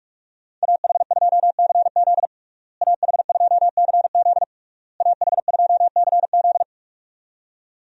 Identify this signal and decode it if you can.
Morse code, heard as a keyed tone, about 46 wpm, 700 Hz. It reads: AH2CZ AH2CZ AH2CZ
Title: USA Call Signs - 5 Characters - Repeated 3x - 45wpm